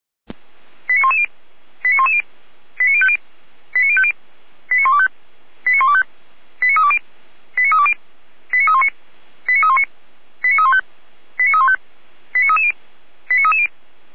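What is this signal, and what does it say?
Five-tone selective call (ZVEI), heard as quick runs of five tones, tones of about 70 ms. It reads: 89110 89110 89050 89050 89125 89125 89320 89320 89319 89319 89315 89315 89330 89330
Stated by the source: Five-tone ZVEI selective call received by SDR.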